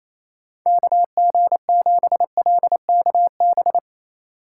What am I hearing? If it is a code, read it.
Morse code, heard as a keyed tone, about 31 wpm, 700 Hz. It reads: KG7LKB
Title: Speed Racing - USA 2x3 Call Signs 28wpm